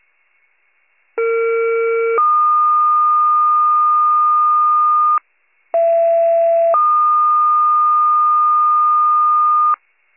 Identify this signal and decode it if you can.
Two-tone page, heard as a one-second tone, then a three-second tone, about 460 then 1220 Hz, about 670 then 1220 Hz. It is heard from 1.2 to 9.7 s.